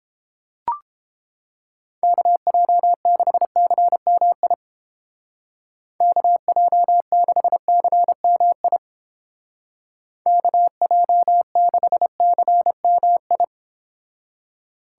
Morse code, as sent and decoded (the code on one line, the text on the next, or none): -.- .--- -.... -.-. -- .. / -.- .--- -.... -.-. -- .. / -.- .--- -.... -.-. -- ..
KJ6CMI KJ6CMI KJ6CMI